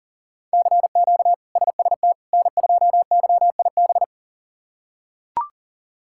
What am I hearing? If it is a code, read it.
Morse code, heard as a keyed tone, about 37 wpm, 700 Hz. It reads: CQ SST N2YIB
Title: K1USN SST QSO Practice - 40wpm